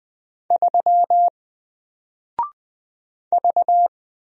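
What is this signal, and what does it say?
Morse code, heard as a keyed tone, about 20 wpm, 700 Hz. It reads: 3 V
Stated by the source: Single letter and number Morse drill at 20 wpm, from N4LCD.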